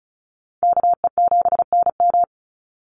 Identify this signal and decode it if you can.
Morse code, heard as a keyed tone, about 34 wpm, 700 Hz. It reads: KE7NM
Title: USA Call Signs 35wpm